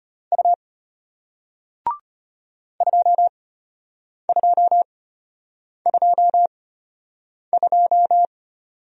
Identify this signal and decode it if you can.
Morse code, heard as a keyed tone, about 33 wpm, spaced as for 8 wpm, 700 Hz. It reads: U 2222